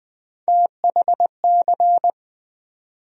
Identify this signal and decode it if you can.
Morse code, heard as a keyed tone, about 20 wpm, 700 Hz. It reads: THC
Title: USA Call Sign Postfixes - 3 Letters 20wpm